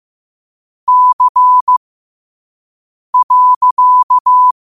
Morse code, heard as a keyed tone, about 15 wpm, spaced as for 5 wpm, 1000 Hz. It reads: C.
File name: qotdfiles / 5wpm